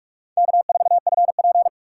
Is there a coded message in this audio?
Yes: Morse code from 0.4 to 1.7 s, heard as a keyed tone, about 41 wpm, 700 Hz. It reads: K4FP